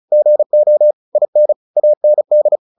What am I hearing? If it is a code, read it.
Morse code, heard as a keyed tone, about 34 wpm, 600 Hz. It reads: GO IN AND
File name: the old man and the sea 2 35wpm